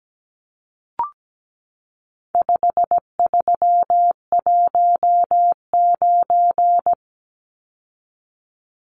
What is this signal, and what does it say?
Morse code, heard as a keyed tone, about 17 wpm, 700 Hz. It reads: 5319
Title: Numbers - 4-Digit 17wpm